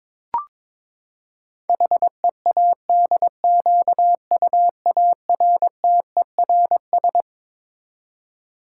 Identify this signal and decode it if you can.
Morse code, heard as a keyed tone, about 22 wpm, 700 Hz. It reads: HEADQUARTERS
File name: Twelve Letter Words - 22wpm